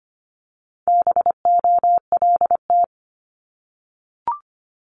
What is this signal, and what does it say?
Morse code, heard as a keyed tone, about 25 wpm, 700 Hz. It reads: BOLT
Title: Multi-letter Phonemes 25wpm